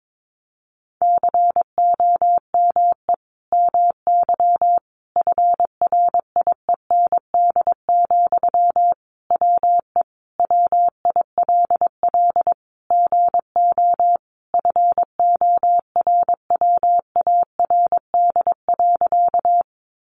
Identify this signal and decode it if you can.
Morse code, heard as a keyed tone, about 22 wpm, 700 Hz. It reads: COME MY FRIEND, WE WILL GO FORWARD.
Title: Sentences from Top 1300 Words 22wpm